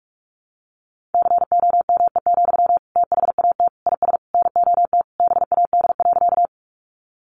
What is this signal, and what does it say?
Morse code, heard as a keyed tone, about 45 wpm, 700 Hz. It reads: COME, THAT IS NOT BAD.
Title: Sentences from Top 700 Words 45wpm